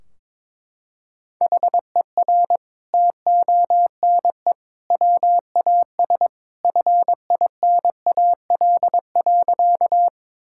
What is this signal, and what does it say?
Morse code, heard as a keyed tone, about 22 wpm, 700 Hz. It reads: HER TONE WAS FINAL.
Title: Sentences from Top 1600 Words 22wpm